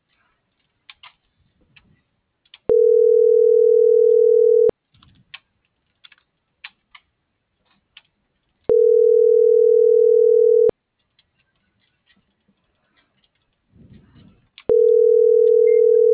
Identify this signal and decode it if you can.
Ringback tone (North America), 440 + 480 Hz.